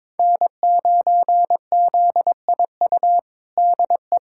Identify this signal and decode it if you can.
Morse code, heard as a keyed tone, about 22 wpm, 700 Hz. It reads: N9ZIU DE
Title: USA Callsign DE Callsign 22wpm